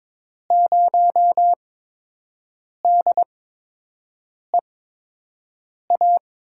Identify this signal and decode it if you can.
Morse code, heard as a keyed tone, about 21 wpm, spaced as for 6 wpm, 700 Hz. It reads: 0DEA
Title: USA Call Signs - 5 Characters - 8x Character Spacing - 22wpm